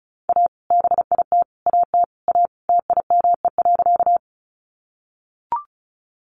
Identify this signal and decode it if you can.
Morse code, heard as a keyed tone, about 34 wpm, 700 Hz. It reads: A BIT AT A TIME.
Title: Sentences from Top 900 Words 35wpm